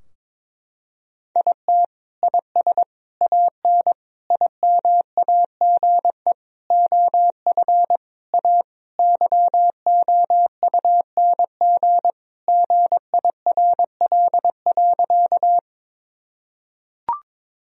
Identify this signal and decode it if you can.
Morse code, heard as a keyed tone, about 22 wpm, 700 Hz. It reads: IT IS AN IMAGE OF A YOUNG GIRL.
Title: Sentences from Top 1000 Words - Review Part 4 - 22wpm